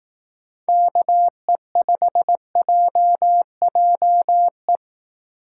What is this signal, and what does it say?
Morse code, heard as a keyed tone, about 18 wpm, 700 Hz. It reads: KE5JJE
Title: Speed Racing - USA 2x3 Call Signs 15wpm